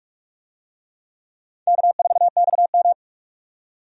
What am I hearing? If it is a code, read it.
Morse code, heard as a keyed tone, about 43 wpm, 700 Hz. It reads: K4XM
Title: USA Call Signs - 4 Characters - 45wpm